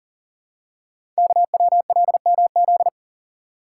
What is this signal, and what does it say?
Morse code, heard as a keyed tone, about 40 wpm, 690 Hz. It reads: KWLMZ
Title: Code Groups - 5 Letter:Numbers - 40wpm